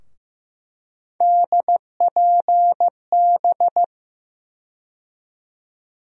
Morse code, heard as a keyed tone, about 15 wpm, 700 Hz. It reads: DPB
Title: USA Call Sign Postfixes - 3 Letters 15wpm